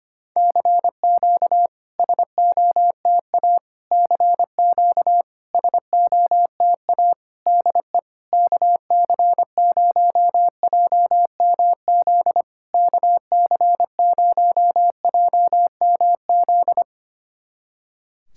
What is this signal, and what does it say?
Morse code, heard as a keyed tone, about 25 wpm, 700 Hz. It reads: CQ SOTA CQ SOTA DE KC0JMZ KC0JMZ
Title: SOTA QSO Practice 25wpm